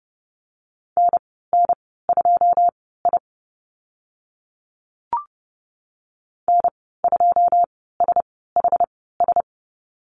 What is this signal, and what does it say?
Morse code, heard as a keyed tone, about 30 wpm, spaced as for 17 wpm, 700 Hz. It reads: NN2I N2SHS
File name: Lesson 14g - Number 2 - Call Signs 30wpm